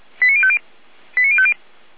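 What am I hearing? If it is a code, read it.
Five-tone selective call (ZVEI), heard as quick runs of five tones, tones of about 80 ms. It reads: 89050 89050
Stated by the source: ZVEI five-tone selective call, address 89187, received on an SDR.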